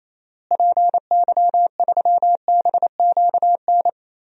Morse code, heard as a keyed tone, about 28 wpm, 700 Hz. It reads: PY3BQN
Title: DXCC-100 Call Signs - 6 Characters - Repeated 2x - 28wpm